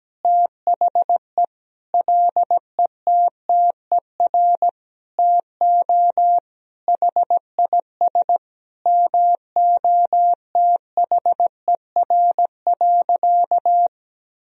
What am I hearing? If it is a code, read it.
Morse code, heard as a keyed tone, about 17 wpm, 700 Hz. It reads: THE LETTER TO HIS MOTHER.
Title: Sentences from Top 1000 Words - Review Part 3 - 17wpm